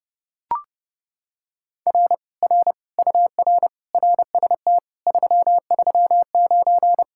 Morse code, heard as a keyed tone, about 28 wpm, 700 Hz. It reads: R R UR RST 339